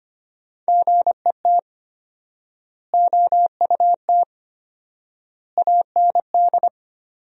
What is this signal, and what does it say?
Morse code, heard as a keyed tone, about 25 wpm, 700 Hz. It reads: GET OUT AND
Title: 4x Word-Spacing - Sentences from Top 200 Words - 25wpm